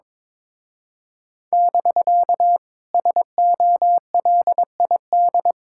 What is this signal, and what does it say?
Morse code, heard as a keyed tone, about 23 wpm, 700 Hz. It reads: <BK> SOLID
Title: Speed-Racing Common QSO Phrases - 22wpm